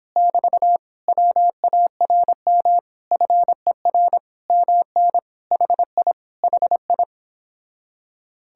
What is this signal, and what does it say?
Morse code, heard as a keyed tone, about 26 wpm, 700 Hz. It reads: <BT> WARM FER MN HI HI